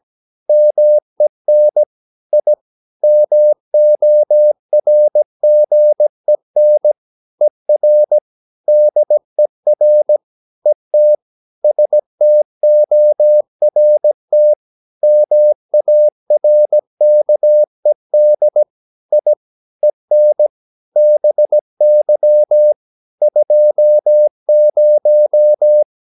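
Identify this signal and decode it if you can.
Morse code, heard as a keyed tone, about 17 wpm, 600 Hz. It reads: MEN I MORGEN ER DER ET STORT MARKED I EN BY 20
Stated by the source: Slow (QRS) Morse practice at 17 wpm, from CW med Gnister.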